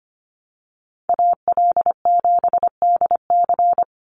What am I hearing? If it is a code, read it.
Morse code, heard as a keyed tone, about 25 wpm, 700 Hz. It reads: AL7DC
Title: USA Call Signs 25wpm